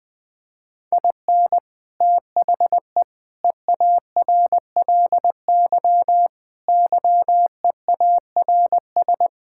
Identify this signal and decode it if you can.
Morse code, heard as a keyed tone, about 20 wpm, 700 Hz. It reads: IN THE EARLY YEARS
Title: Sets of 6 Words - 20wpm